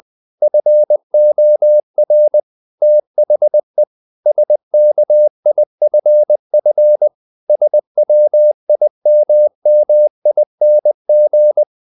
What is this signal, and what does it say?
Morse code, heard as a keyed tone, about 20 wpm, 600 Hz. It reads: FOR THE SKIFF SWIMMING
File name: the old man and the sea part 2 20wpm